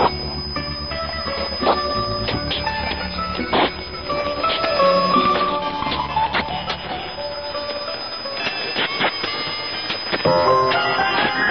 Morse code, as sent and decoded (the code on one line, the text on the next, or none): none